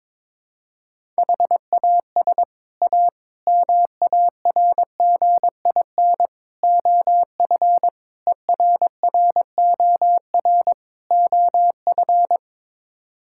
Morse code, heard as a keyed tone, about 22 wpm, 700 Hz. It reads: HAS A MARGIN OF ERROR OF
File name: practice / 22wpm / Sets of 6 Words - 22wpm